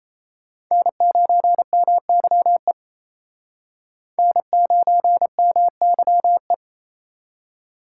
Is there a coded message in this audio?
Yes: Morse code from 0.7 to 6.5 s, heard as a keyed tone, about 36 wpm, 700 Hz. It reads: N9MYE N9MYE